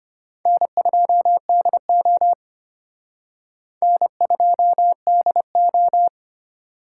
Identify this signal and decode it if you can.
Morse code, heard as a keyed tone, about 33 wpm, 700 Hz. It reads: N2DO N2DO